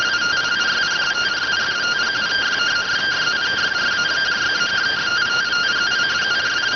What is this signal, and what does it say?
Two-tone teleprinter signal, about 170 Hz shift.